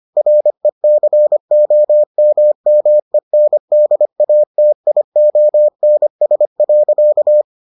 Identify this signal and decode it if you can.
Morse code, heard as a keyed tone, about 25 wpm, 600 Hz. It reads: RECOMMENDATIONS.